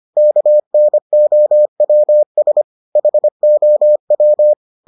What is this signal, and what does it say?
Morse code, heard as a keyed tone, about 25 wpm, 600 Hz. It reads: KNOWS HOW